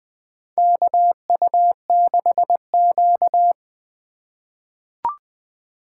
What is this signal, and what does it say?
Morse code, heard as a keyed tone, about 20 wpm, 700 Hz. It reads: KU6Q
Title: USA Call Signs - 3x Character Spacing - 20wpm